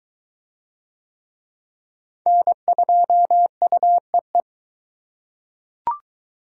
Morse code, heard as a keyed tone, about 22 wpm, 700 Hz. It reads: N2UEE